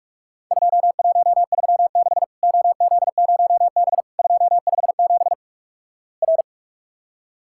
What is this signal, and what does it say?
Morse code, heard as a keyed tone, about 46 wpm, 700 Hz. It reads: 2136 OZ0B 257 R
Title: Worked All Europe DX Contest - QTC Exchange - 45wpm